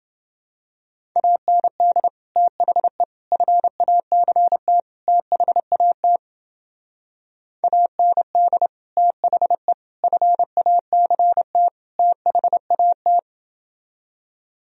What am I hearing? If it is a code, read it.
Morse code, heard as a keyed tone, about 29 wpm, 700 Hz. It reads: AND THE FACT THAT AND THE FACT THAT